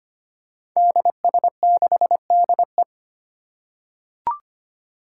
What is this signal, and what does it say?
Morse code, heard as a keyed tone, about 25 wpm, 700 Hz. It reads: DS6DE
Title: Five Challenging Letter-Number Combinations 25wpm